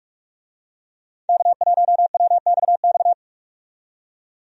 Morse code, heard as a keyed tone, about 42 wpm, 700 Hz. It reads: K1WXX
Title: USA Call Signs - 5 Characters - 45wpm